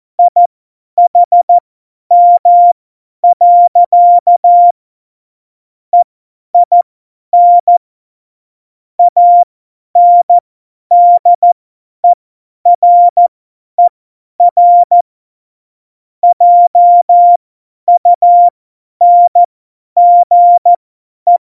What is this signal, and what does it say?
Morse code, heard as a keyed tone, about 14 wpm, spaced as for 10 wpm, 700 Hz. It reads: IHM. EIN ANDERER JUNGE
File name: Morsekurs / Witze / Witz 27. Oktober 2020, 50 F70